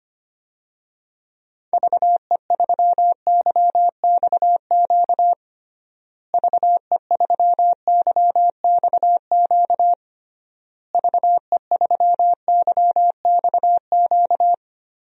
Morse code, heard as a keyed tone, about 25 wpm, 700 Hz. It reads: VE3YXQ VE3YXQ VE3YXQ
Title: DXCC-100 Call Signs - 6 Characters - Repeated 3x - 25wpm